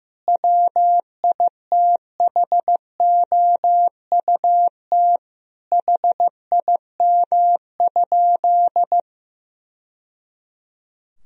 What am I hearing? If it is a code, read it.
Morse code, heard as a keyed tone, about 16 wpm, 700 Hz. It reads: WITHOUT HIM?
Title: Sentences from Top 1000 Words - Review Part 3 - 15wpm